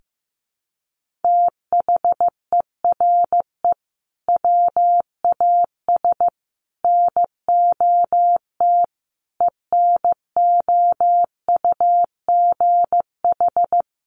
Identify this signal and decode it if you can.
Morse code, heard as a keyed tone, about 15 wpm, 700 Hz. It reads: THERE WAS NOT ENOUGH
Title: Sentences from Top 1400 Words 15wpm